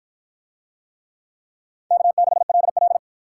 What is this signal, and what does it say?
Morse code, heard as a keyed tone, about 53 wpm, 700 Hz. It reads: K6LL